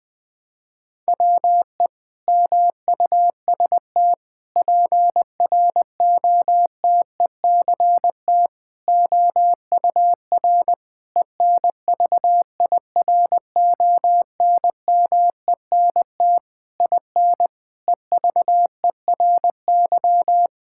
Morse code, heard as a keyed tone, about 20 wpm, 700 Hz. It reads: WE MUST PROTECT OUR ENVIRONMENT IN EVERY
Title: Sentences from Top 1000 Words - Review Part 5 - 20wpm